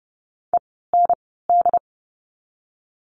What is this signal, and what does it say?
Morse code, heard as a keyed tone, about 30 wpm, 700 Hz. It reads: E N D
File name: Lesson 16f - Letter c - Words 30wpm